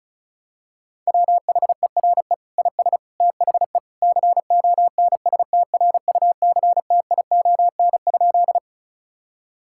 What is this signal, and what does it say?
Morse code, heard as a keyed tone, about 34 wpm, 700 Hz. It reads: WHERE IS THE CONSTRUCTION?
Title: Sentences from Top 1600 Words 35wpm